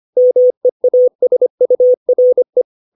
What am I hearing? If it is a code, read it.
Morse code, heard as a keyed tone, about 25 wpm, 500 Hz. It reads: MEASURE
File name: 632 words / 632 Words 25WPM